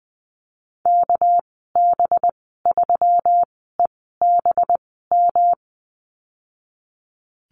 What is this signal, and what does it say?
Morse code, heard as a keyed tone, about 20 wpm, spaced as for 15 wpm, 700 Hz. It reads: KB3EBM
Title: 231 Lesson 37g - Letter x - Call Signs 20wpm